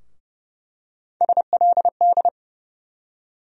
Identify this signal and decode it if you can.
Morse code, heard as a keyed tone, about 30 wpm, 700 Hz. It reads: SLD
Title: USA Call Sign Postfixes - 3 Letters 30wpm